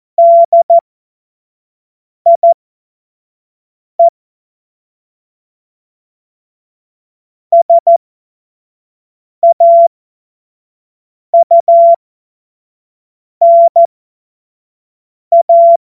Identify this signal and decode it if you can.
Morse code, heard as a keyed tone, about 14 wpm, spaced as for 5 wpm, 680 Hz. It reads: DIE SAUNA
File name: Morsekurs / Witze / Witz 13. Juli 2021, 25 F70